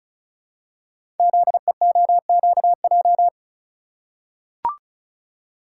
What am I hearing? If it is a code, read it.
Morse code, heard as a keyed tone, about 36 wpm, 700 Hz. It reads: ZEOQJ
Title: Code Groups - 5 Letters - 35wpm